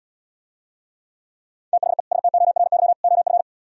Morse code, heard as a keyed tone, about 77 wpm, 700 Hz. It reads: THE STORY OF